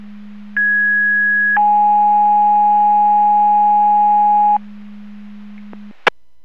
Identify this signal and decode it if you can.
Two-tone page, heard as a one-second tone, then a three-second tone, about 1650 then 830 Hz. It is heard from 0.6 to 4.6 s.
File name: Portage 154.385 Paging